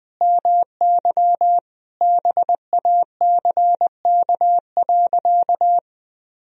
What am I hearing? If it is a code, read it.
Morse code, heard as a keyed tone, about 21 wpm, 700 Hz. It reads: MY BACK.
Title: Sentences from Top 1000 Words - Review Part 1 - 20wpm